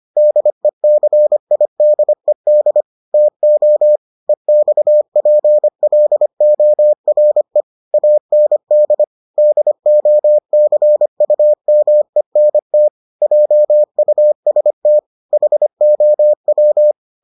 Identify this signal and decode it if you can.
Morse code, heard as a keyed tone, about 25 wpm, 600 Hz. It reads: DECIDED TO EXPLORE AND DOCUMENT JUST HOW